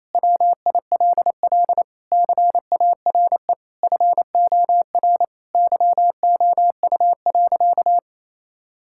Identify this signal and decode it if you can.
Morse code, heard as a keyed tone, about 28 wpm, 700 Hz. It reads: WILL CARE FOR YOU.
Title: Sentences from Top 1000 Words - Review Part 6 - 28wpm